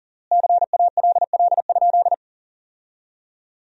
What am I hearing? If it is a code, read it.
Morse code, heard as a keyed tone, about 37 wpm, 700 Hz. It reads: CALL?
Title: 3x Word-Spacing - Sentences from Top 100 Words - 40wpm